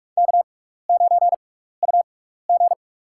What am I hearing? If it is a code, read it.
Morse code, heard as a keyed tone, about 45 wpm, spaced as for 15 wpm, 700 Hz. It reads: K9UG
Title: USA Call Signs - 4 Characters - 6x Character Spacing - 45wpm